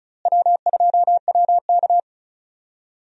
Morse code, heard as a keyed tone, about 36 wpm, 700 Hz. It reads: W2WK
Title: USA Call Signs - 4 Characters - Repeated 3x - 35wpm